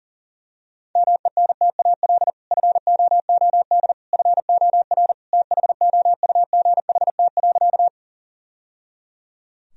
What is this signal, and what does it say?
Morse code, heard as a keyed tone, about 40 wpm, 700 Hz. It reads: MENTAL FOOD FOR THOUGHT.
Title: Sentences from Top 1000 Words - Review Part 4 - 40wpm